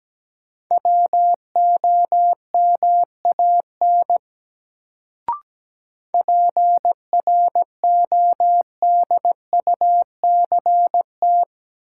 Morse code, heard as a keyed tone, about 17 wpm, 700 Hz. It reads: WOMAN PRODUCT
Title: Sentences from Top 1000 Words - Review Part 2 - New Word Drill - 17wpm